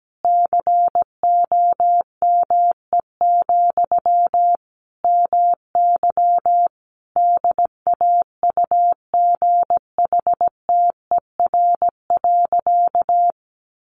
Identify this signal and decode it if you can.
Morse code, heard as a keyed tone, about 17 wpm, 700 Hz. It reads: COME, MY DAUGHTER.
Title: Sentences from Top 1500 Words 17wpm